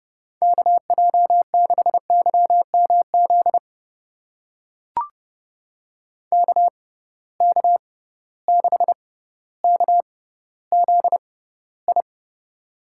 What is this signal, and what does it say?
Morse code, heard as a keyed tone, about 30 wpm, spaced as for 15 wpm, 700 Hz. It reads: KJ6YMZ KK6KZI
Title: USA Call Signs - 6 Characters - 6x Character Spacing - 30wpm